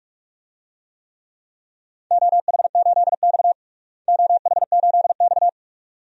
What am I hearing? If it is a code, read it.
Morse code, heard as a keyed tone, about 46 wpm, 700 Hz. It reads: OH8X OH8X
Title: DXCC-100 Call Signs - 4 Characters - Repeated 2x - 45wpm